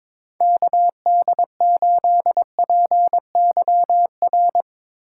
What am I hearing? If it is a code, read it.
Morse code, heard as a keyed tone, about 22 wpm, 700 Hz. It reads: KD8PYR
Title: Speed Racing - USA 2x3 Call Signs 22wpm